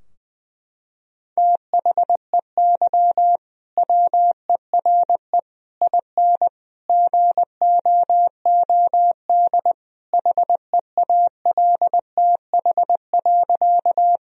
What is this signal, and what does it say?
Morse code, heard as a keyed tone, about 21 wpm, 700 Hz. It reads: THEY WERE IN GOOD HEALTH.
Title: Sentences from Top 1000 Words - Review Part 3 - 20wpm